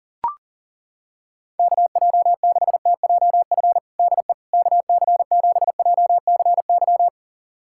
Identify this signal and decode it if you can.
Morse code, heard as a keyed tone, about 40 wpm, 700 Hz. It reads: KJ6TJF DE KC7JCY
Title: USA Callsign DE Callsign - 6 Characters - 40wpm